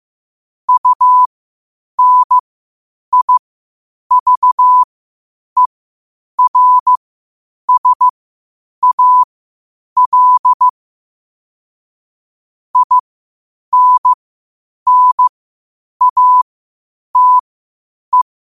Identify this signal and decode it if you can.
Morse code, heard as a keyed tone, about 15 wpm, spaced as for 8 wpm, 1000 Hz. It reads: UNIVERSAL INNATE
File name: qotdfiles / 8wpm